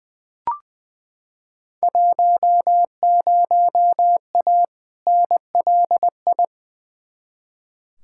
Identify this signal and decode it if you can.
Morse code, heard as a keyed tone, about 20 wpm, 700 Hz. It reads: 10A NLI
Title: Field Day Exchanges Encore - 20wpm